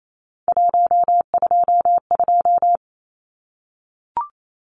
Morse code, heard as a keyed tone, about 28 wpm, 700 Hz. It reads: 122
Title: Numbers - 3-Digit 28wpm